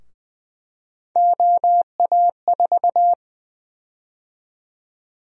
Morse code, heard as a keyed tone, about 20 wpm, 700 Hz. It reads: OA4
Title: USA Call Sign Prefixes - 20wpm